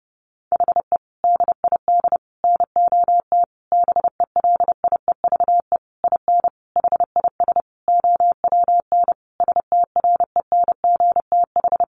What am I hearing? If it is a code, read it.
Morse code, heard as a keyed tone, about 30 wpm, 700 Hz. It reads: HE DID NOT BELIEVE IN HIS OWN STRENGTH
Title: Sentences from Top 1300 Words 30wpm